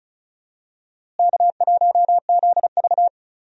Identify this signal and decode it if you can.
Morse code, heard as a keyed tone, about 36 wpm, 700 Hz. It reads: K1ZV
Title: USA Call Signs - 4 Characters - Repeated 2x - 35wpm